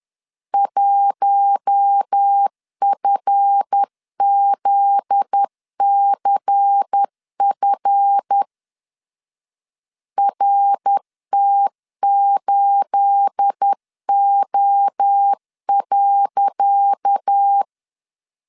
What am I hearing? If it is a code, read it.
Morse code, heard as a keyed tone, about 10 wpm, 800 Hz. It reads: 1FZCF RT8O.
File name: LEZIONI / 10 SIGNS / 10C DOT 10wpm 2940